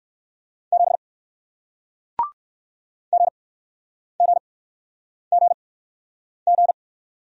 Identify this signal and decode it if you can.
Morse code, heard as a keyed tone, about 60 wpm, spaced as for 9 wpm, 700 Hz. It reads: 7 GGGG